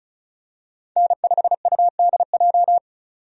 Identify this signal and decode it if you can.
Morse code, heard as a keyed tone, about 36 wpm, 700 Hz. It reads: N5UDJ
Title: USA Call Signs - 5 Characters - 35wpm